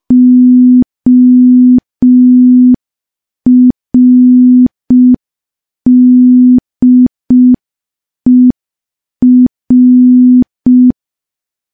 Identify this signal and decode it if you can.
Morse code, heard as a keyed tone, about 5 wpm, 250 Hz. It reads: ORDER
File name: morse / morse-training-05WPM